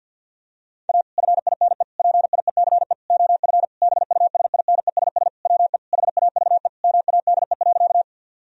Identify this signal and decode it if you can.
Morse code, heard as a keyed tone, about 51 wpm, 700 Hz. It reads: A FINE PIECE OF BUSINESS WE HAVE MADE.
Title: Sentences from Top 1000 Words - Review Part 2 - 50wpm